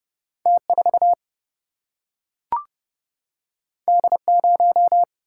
Morse code, heard as a keyed tone, about 31 wpm, 700 Hz. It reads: T 4 D 0
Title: Single-Letter Single-Number 30wpm